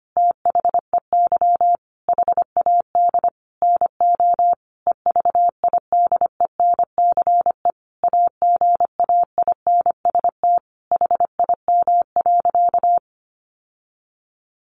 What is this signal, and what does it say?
Morse code, heard as a keyed tone, about 25 wpm, 700 Hz. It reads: THEY HAD NO EVIDENCE AGAINST HIM.
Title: Sentences from Top 1300 Words 25wpm